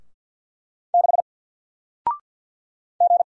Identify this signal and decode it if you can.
Morse code, heard as a keyed tone, about 50 wpm, 700 Hz. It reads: 6 G
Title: Single Character (Letter:Number:Punctuation:Prosign) Alt 50wpm